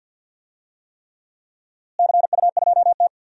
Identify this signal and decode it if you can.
Morse code, heard as a keyed tone, about 51 wpm, 700 Hz. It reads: CU2T
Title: DXCC-100 Call Signs - 4 Characters - Repeated 3x - 50wpm